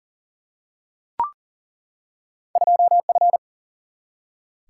Morse code, heard as a keyed tone, about 38 wpm, 700 Hz. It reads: 2F
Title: Single-Letter Single-Number 40wpm